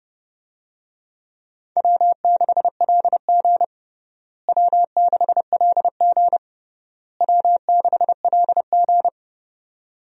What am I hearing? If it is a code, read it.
Morse code, heard as a keyed tone, about 32 wpm, 700 Hz. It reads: W6LG W6LG W6LG
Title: USA Call Signs - 4 Characters - Repeated 3x - 30wpm